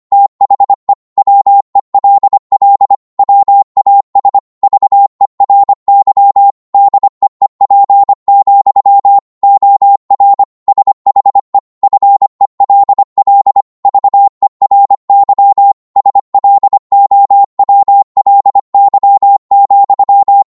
Morse code, heard as a keyed tone, about 25 wpm, 800 Hz. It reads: THE WELL WAS VERY DEEP, OR SHE FELL VERY SLOWLY,